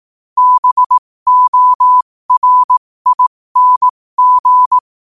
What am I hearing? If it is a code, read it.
Morse code, heard as a keyed tone, about 18 wpm, spaced as for 15 wpm, 1000 Hz. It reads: BORING
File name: qotdfiles / 15wpm